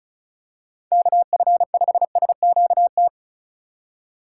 Morse code, heard as a keyed tone, about 36 wpm, 700 Hz. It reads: KF5SQT
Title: USA Call Signs - 3x Character Spacing - 35wpm